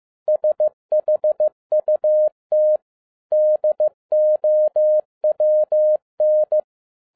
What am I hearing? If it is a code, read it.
Morse code, heard as a keyed tone, about 15 wpm, 610 Hz. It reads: SHUT DOWN